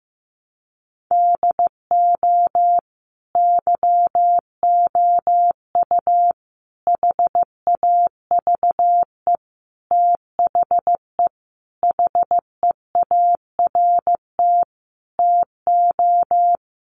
Morse code, heard as a keyed tone, about 15 wpm, 700 Hz. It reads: DO YOU HAVE THE HEART TO
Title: Sentences from Top 1200 Words 15wpm